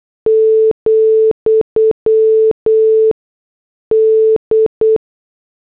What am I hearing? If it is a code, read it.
Morse code, heard as a keyed tone, about 8 wpm, 440 Hz. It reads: , D